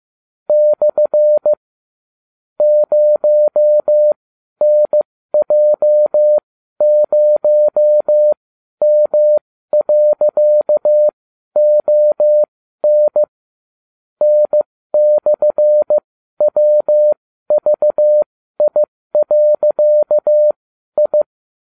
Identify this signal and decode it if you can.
Morse code, heard as a keyed tone, about 15 wpm, spaced as for 12 wpm, 600 Hz. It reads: / 0NJ0M.ON N/WVI.I